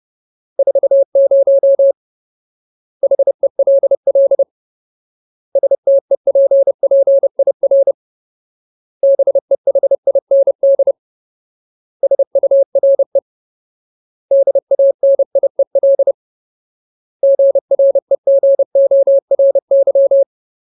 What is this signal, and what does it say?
Morse code, heard as a keyed tone, about 30 wpm, 550 Hz. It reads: 40 HELL STEPPIR BEHIND SURE DANIEL GREGORY
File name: Ultimate Word List / 30WPM Ultimate